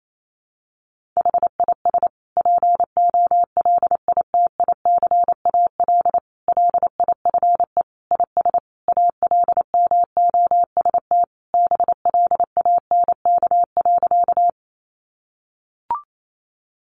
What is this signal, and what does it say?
Morse code, heard as a keyed tone, about 28 wpm, 700 Hz. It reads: HIS POLITICAL LIFE IS ALMOST BLANK.
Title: Sentences from Top 1500 Words 28wpm